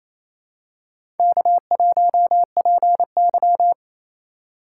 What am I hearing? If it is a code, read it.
Morse code, heard as a keyed tone, about 28 wpm, 700 Hz. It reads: K1PY
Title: USA Call Signs - 4 Characters - 2x Character Spacing - 28wpm